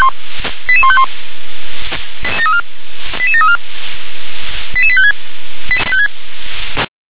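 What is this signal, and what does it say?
Five-tone selective call (ZVEI), heard as quick runs of five tones, tones of about 70 ms. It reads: 88151 88734 88856
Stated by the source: ZVEI five-tone selective call received on SDR.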